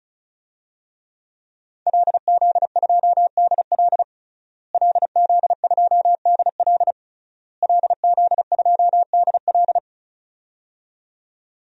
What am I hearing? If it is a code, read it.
Morse code, heard as a keyed tone, about 34 wpm, 700 Hz. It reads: LZ2DL LZ2DL LZ2DL
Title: DXCC-100 Call Signs - 5 Characters - Repeated 3x - 35wpm